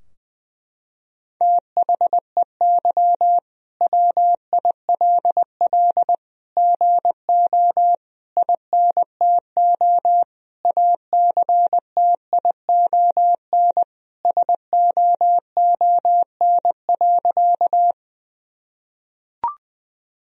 Morse code, heard as a keyed tone, about 20 wpm, 700 Hz. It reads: THEY WILL GO INTO ACTION SOON.
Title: Sentences from Top 1000 Words - Review Part 4 - 20wpm